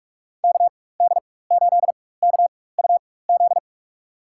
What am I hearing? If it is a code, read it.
Morse code, heard as a keyed tone, about 46 wpm, spaced as for 21 wpm, 700 Hz. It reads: KD8KUZ
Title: USA Call Signs - 6 Characters - 4x Character Spacing - 45wpm